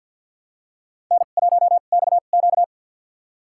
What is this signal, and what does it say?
Morse code, heard as a keyed tone, about 46 wpm, spaced as for 35 wpm, 700 Hz. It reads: N1XQ
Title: USA Call Signs - 2x Character Spacing - 50wpm